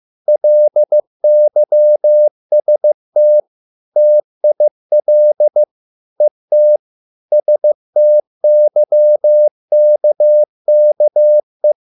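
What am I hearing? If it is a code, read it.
Morse code, heard as a keyed tone, about 15 wpm, 600 Hz. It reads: LYST TIL ET STYKKE
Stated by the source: Morse code at a slow 15 wpm from CW med Gnister.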